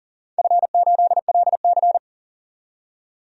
Morse code, heard as a keyed tone, about 40 wpm, 700 Hz. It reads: F8LC
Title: DXCC-100 Call Signs - 4 Characters - 40wpm